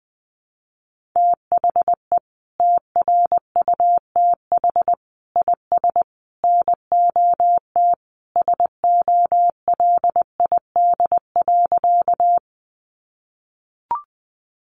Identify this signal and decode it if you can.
Morse code, heard as a keyed tone, about 20 wpm, 700 Hz. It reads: THE TRUTH IS NOT SOLID.
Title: Sentences from Top 1500 Words 20wpm